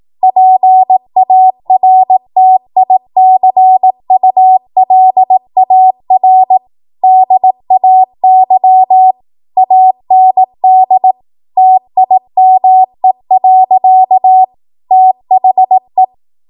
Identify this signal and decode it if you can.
Morse code, heard as a keyed tone, about 18 wpm, 750 Hz. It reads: PARTICULAR DAY AND TIME. THE
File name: Archive / 18 WPM / 18WPM